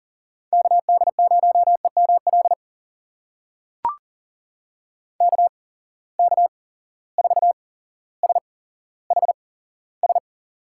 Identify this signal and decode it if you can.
Morse code, heard as a keyed tone, about 40 wpm, spaced as for 17 wpm, 700 Hz. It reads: KD0EML KK4SHS
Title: USA Call Signs - 6 Characters - 8x Character Spacing - 40wpm